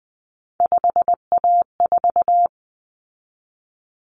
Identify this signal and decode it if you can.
Morse code, heard as a keyed tone, about 20 wpm, 700 Hz. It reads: 5A4
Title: Common Cut Numbers - 3-Digit 20wpm